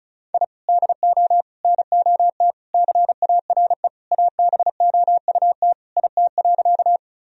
Morse code, heard as a keyed tone, about 36 wpm, 700 Hz. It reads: I DO NOT CARE ABOUT IT.
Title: Sentences from Top 1000 Words - Review Part 1 - 35wpm